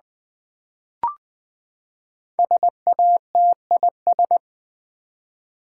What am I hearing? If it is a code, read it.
Morse code, heard as a keyed tone, about 20 wpm, 700 Hz. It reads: SATIS
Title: Build A Word - 5 Letters At A Time - 12-Letter Words - 20wpm